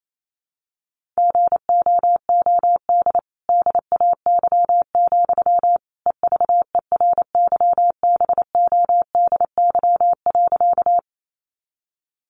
Morse code, heard as a keyed tone, about 28 wpm, 700 Hz. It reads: GOOD DAY, EVERYBODY.
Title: Sentences from Top 1400 Words 28wpm